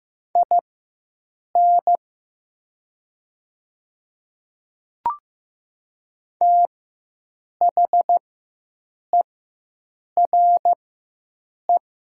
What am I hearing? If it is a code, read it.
Morse code, heard as a keyed tone, about 15 wpm, spaced as for 7 wpm, 700 Hz. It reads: IN THERE
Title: 4x Character Spacing - Top 100 Words - 15wpm